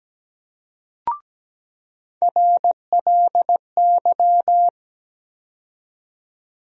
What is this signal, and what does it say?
Morse code, heard as a keyed tone, about 17 wpm, 700 Hz. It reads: RLY
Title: USA Call Sign Postfixes - 3 Letters 17wpm